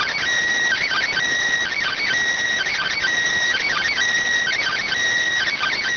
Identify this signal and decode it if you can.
Emergency Alert System (SAME) tones, heard as harsh buzzing bursts. It is heard from 0.7 to 4.9 s.